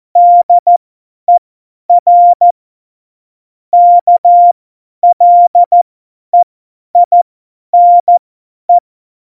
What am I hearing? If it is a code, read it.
Morse code, heard as a keyed tone, about 14 wpm, spaced as for 10 wpm, 700 Hz. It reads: DER KLEINE